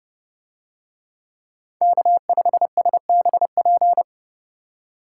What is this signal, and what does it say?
Morse code, heard as a keyed tone, about 30 wpm, 700 Hz. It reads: K5SBP